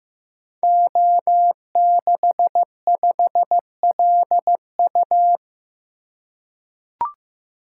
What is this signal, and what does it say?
Morse code, heard as a keyed tone, about 15 wpm, 700 Hz. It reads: O65LU
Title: Code Groups - 5 Letter:Numbers - 15wpm